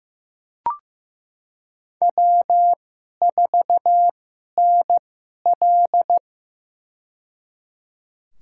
Morse code, heard as a keyed tone, about 15 wpm, spaced as for 11 wpm, 700 Hz. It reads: W4NL
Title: USA Call Signs - 2x Character Spacing - 15wpm